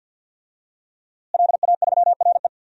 Morse code, heard as a keyed tone, about 51 wpm, 700 Hz. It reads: LA3RE
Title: DXCC-100 Call Signs - 5 Characters - Repeated 2x - 50wpm